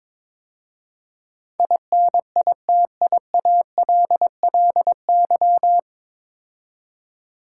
Morse code, heard as a keyed tone, about 22 wpm, 700 Hz. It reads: INITIALLY
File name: 9 Letter Words - First 3 Letters Given - 22wpm